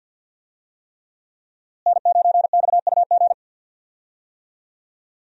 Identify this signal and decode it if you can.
Morse code, heard as a keyed tone, about 53 wpm, 700 Hz. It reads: N9XUG
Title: USA Call Signs - 5 Characters - Repeated 3x - 50wpm